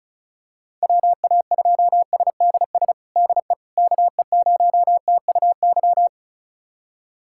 Morse code, heard as a keyed tone, about 34 wpm, 700 Hz. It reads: WA2SDS DE KE0TUY
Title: USA Callsign DE Callsign - 6 Characters - 35wpm